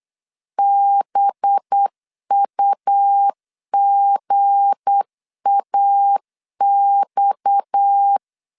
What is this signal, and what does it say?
Morse code, heard as a keyed tone, about 8 wpm, 800 Hz. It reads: BUGAX
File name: LEZIONI / 8 YJ45 / 8 YJ45 8wpm 3214